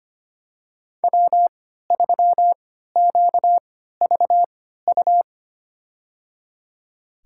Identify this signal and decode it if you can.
Morse code, heard as a keyed tone, about 25 wpm, spaced as for 14 wpm, 700 Hz. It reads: W3QVU
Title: USA Call Signs - 5 Characters - 3x Character Spacing - 25wpm